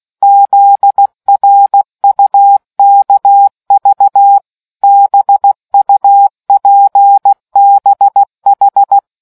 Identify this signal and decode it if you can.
Morse code, heard as a keyed tone, about 16 wpm, 800 Hz. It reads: ZRUKV BUPBH